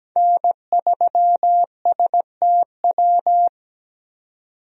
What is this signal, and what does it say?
Morse code, heard as a keyed tone, about 17 wpm, 700 Hz. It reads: N3STW
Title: USA Call Signs - 5 Characters - 17wpm